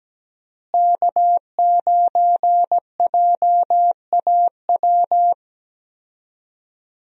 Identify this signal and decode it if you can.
Morse code, heard as a keyed tone, about 17 wpm, 700 Hz. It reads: K9JAW
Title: USA Call Signs - 5 Characters - Repeated 3x - 17wpm